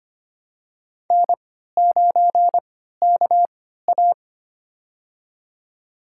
Morse code, heard as a keyed tone, about 25 wpm, spaced as for 14 wpm, 700 Hz. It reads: N9KA
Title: USA Call Signs - 4 Characters - 3x Character Spacing - 25wpm